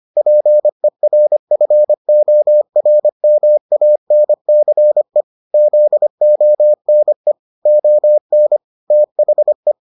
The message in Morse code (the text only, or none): PERFORMANCE ZONE ON THE